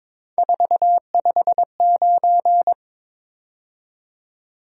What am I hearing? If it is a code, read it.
Morse code, heard as a keyed tone, about 26 wpm, 700 Hz. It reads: 459
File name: Speed-Racing - Three Letter-Number Combinations - 22wpm